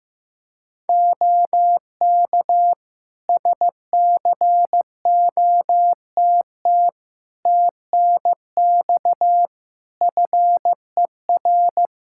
Morse code, heard as a keyed tone, about 14 wpm, 700 Hz. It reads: OK SCOTT TNX FER
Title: Practice QSO - Exchange 4 - Ending - 15wpm